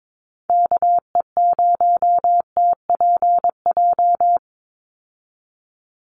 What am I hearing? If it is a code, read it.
Morse code, heard as a keyed tone, about 22 wpm, 700 Hz. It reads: KE0TPJ
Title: USA Call Signs 22wpm